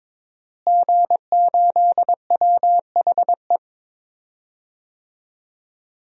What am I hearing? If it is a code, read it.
Morse code, heard as a keyed tone, about 23 wpm, 700 Hz. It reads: G8WHE
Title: DXCC-100 Call Signs - 5 Characters - Repeated 2x - 22wpm